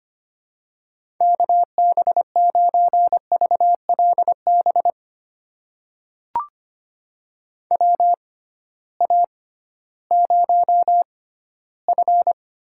Morse code, heard as a keyed tone, about 25 wpm, spaced as for 15 wpm, 700 Hz. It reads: KB9VLB WA0F
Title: USA Call Signs - 6x Character Spacing - 25wpm